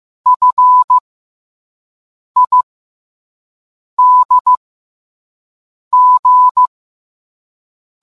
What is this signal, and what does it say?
Morse code, heard as a keyed tone, about 15 wpm, spaced as for 5 wpm, 1000 Hz. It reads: FIDG